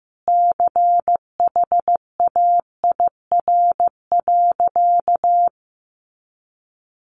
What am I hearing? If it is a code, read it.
Morse code, heard as a keyed tone, about 15 wpm, 700 Hz. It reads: CHAIR.
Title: A Sound Of Thunder 15wpm